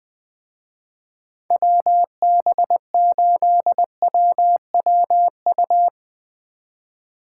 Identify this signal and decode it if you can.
Morse code, heard as a keyed tone, about 22 wpm, 700 Hz. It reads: WB8WWU
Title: Speed Racing - USA 2x3 Call Signs 20wpm